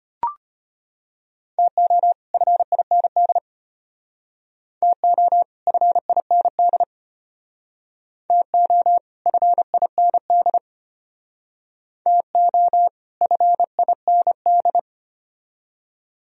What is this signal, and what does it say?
Morse code, heard as a keyed tone, about 32 wpm, 700 Hz. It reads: TO FIND TO FIND TO FIND TO FIND